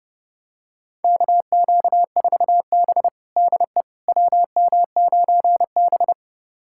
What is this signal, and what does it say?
Morse code, heard as a keyed tone, about 30 wpm, 700 Hz. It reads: KQ4B DE WM9B
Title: USA Callsign DE Callsign - 4 Characters - 30wpm